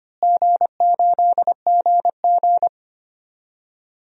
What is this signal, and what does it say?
Morse code, heard as a keyed tone, about 25 wpm, 700 Hz. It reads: G8GG
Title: DXCC-100 Call Signs - 4 Characters - Repeated 2x - 25wpm